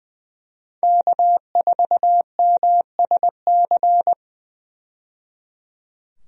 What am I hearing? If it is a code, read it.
Morse code, heard as a keyed tone, about 20 wpm, 700 Hz. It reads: K4MSC
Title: USA Call Signs - 5 Characters - 20wpm